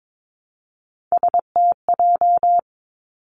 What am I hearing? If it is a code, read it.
Morse code, heard as a keyed tone, about 22 wpm, 700 Hz. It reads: STJ
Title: Three Challenging Letter-Number Combinations 22wpm